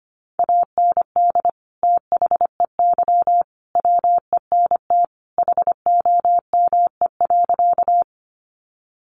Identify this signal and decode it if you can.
Morse code, heard as a keyed tone, about 25 wpm, 700 Hz. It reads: AND THEY WENT HOME.
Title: Sentences from Top 200 Words 25wpm